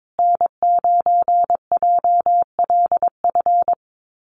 Morse code, heard as a keyed tone, about 22 wpm, 700 Hz. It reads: N9JLF
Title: USA Call Signs Repeated 2x 22wpm